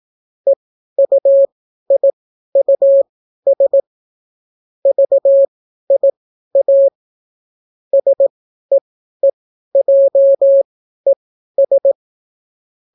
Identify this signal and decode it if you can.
Morse code, heard as a keyed tone, about 18 wpm, spaced as for 12 wpm, 550 Hz. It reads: EUIUS VIA SEEJES